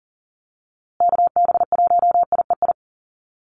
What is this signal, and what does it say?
Morse code, heard as a keyed tone, about 40 wpm, 700 Hz. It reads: KB1IEI
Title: USA Call Signs Repeated 2x 40wpm